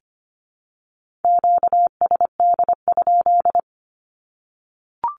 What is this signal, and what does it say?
Morse code, heard as a keyed tone, about 25 wpm, 700 Hz. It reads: QSD?
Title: Q-Codes 25wpm